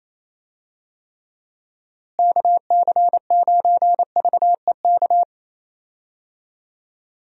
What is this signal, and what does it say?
Morse code, heard as a keyed tone, about 28 wpm, 700 Hz. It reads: KC9VEK
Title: USA Call Signs - 6 Characters - Repeated 3x - 28wpm